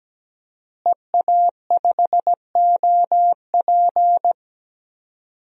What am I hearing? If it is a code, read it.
Morse code, heard as a keyed tone, about 17 wpm, 700 Hz. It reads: EA5OP
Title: DXCC-100 Call Signs - 5 Characters - Repeated 2x - 17wpm